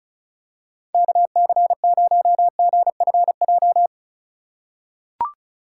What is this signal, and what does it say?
Morse code, heard as a keyed tone, about 36 wpm, 700 Hz. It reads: KC0GFJ
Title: USA Call Signs - 6 Characters - 2x Character Spacing - 35wpm